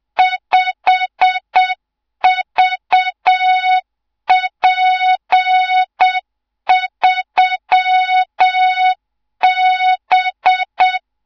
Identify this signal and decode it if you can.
Morse code, heard as a keyed tone, about 7 wpm, 3700 Hz, with noise. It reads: 5VP3B
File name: Morze-HA2MN / hang / 24 GYAKORLAT